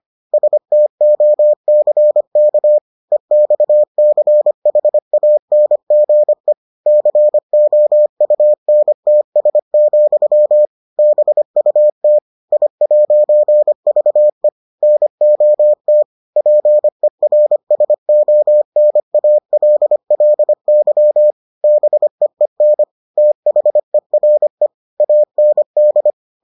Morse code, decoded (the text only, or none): STOCK EXCHANGE COUNTS, BUT I'VE NOT PERSONALLY BEEN THERE AND